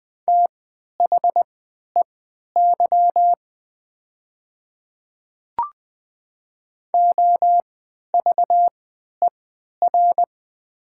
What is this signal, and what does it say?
Morse code, heard as a keyed tone, about 20 wpm, spaced as for 11 wpm, 700 Hz. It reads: THEY OVER
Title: 3x Character Spacing - Top 100 Words - 20wpm